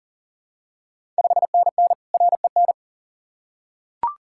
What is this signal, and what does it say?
Morse code, heard as a keyed tone, about 40 wpm, 690 Hz. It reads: 5NN REN
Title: Ontario QSO Party Exchange 40wpm